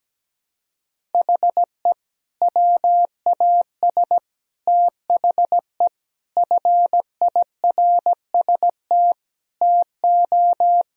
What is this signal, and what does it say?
Morse code, heard as a keyed tone, about 17 wpm, 700 Hz. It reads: HE WAS THE FIRST TO
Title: Sentences from Top 1000 Words - Review Part 1 - 17wpm